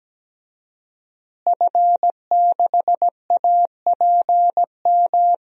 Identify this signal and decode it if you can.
Morse code, heard as a keyed tone, about 17 wpm, 700 Hz. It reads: F6APM
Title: DXCC-100 Call Signs - 5 Characters - Repeated 3x - 17wpm